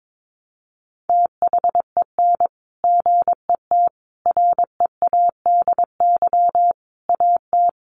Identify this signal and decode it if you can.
Morse code, heard as a keyed tone, about 22 wpm, 700 Hz. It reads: THEN GET READY AT
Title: Sentences from Top 400 Words 22wpm